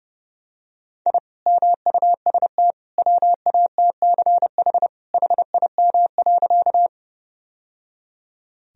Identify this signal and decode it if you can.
Morse code, heard as a keyed tone, about 30 wpm, 700 Hz. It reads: I MUST WATCH HIM.
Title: Sentences from Top 1000 Words - Review Part 1 - 30wpm